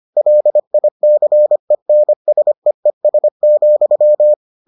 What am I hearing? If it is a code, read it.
Morse code, heard as a keyed tone, about 25 wpm, 600 Hz. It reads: LICENSEES,